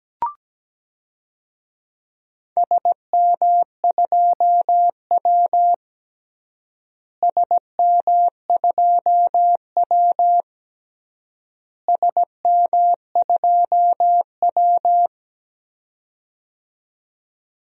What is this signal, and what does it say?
Morse code, heard as a keyed tone, about 17 wpm, 700 Hz. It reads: SM2W SM2W SM2W
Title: DXCC-100 Call Signs - 4 Characters - Repeated 3x - 17wpm